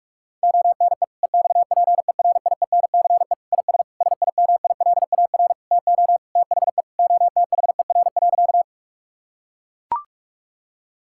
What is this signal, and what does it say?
Morse code, heard as a keyed tone, about 42 wpm, 700 Hz. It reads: ONE EXPERIENCE IS SIMILAR TO THE OTHER.
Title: Sentences from Top 1000 Words - Review Part 5 - 45wpm